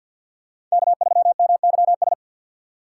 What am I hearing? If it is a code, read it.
Morse code, heard as a keyed tone, about 50 wpm, 690 Hz. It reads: K3MYS
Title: USA Call Signs - 5 Characters - 6x Character Spacing - 50wpm